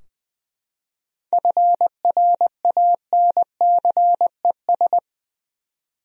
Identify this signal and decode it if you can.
Morse code, heard as a keyed tone, about 20 wpm, 700 Hz. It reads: FRANCES
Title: USA Common Names - 20wpm